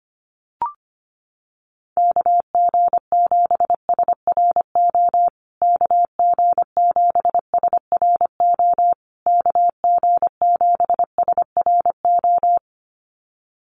Morse code, heard as a keyed tone, about 25 wpm, 700 Hz. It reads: KG7SRO KG7SRO KG7SRO
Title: USA Call Signs Repeated 3x 25wpm